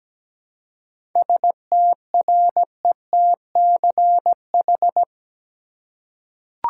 Morse code, heard as a keyed tone, about 17 wpm, 700 Hz. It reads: STRETCH
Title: Seven Letter Words - 17wpm